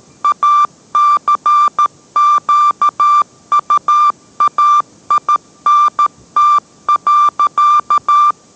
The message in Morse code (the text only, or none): ACQUAINT.